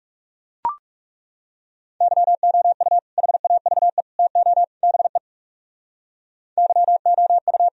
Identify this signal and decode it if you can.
Morse code, heard as a keyed tone, about 36 wpm, 700 Hz. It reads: YOU HAVE TO BE YOU
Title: Speed-Racing - Sets of 4 Words 30wpm